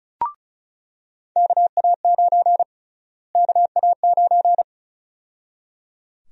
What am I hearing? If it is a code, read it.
Morse code, heard as a keyed tone, about 36 wpm, 700 Hz. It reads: KA9 KA9
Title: USA Call Sign Prefixes - Repeated 2x - 35wpm